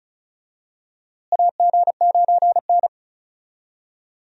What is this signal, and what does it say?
Morse code, heard as a keyed tone, about 36 wpm, 700 Hz. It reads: AG9N
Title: USA Call Signs - 4 Characters - Repeated 3x - 35wpm